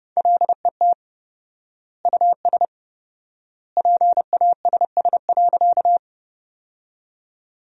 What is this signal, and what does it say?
Morse code, heard as a keyed tone, about 29 wpm, 700 Hz. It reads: LET US PASS.
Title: 4x Word-Spacing - Sentences from Top 400 Words - 30wpm